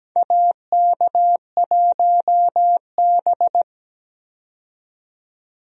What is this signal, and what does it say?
Morse code, heard as a keyed tone, about 17 wpm, 700 Hz. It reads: AK1B
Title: USA Call Signs - 4 Characters - 17wpm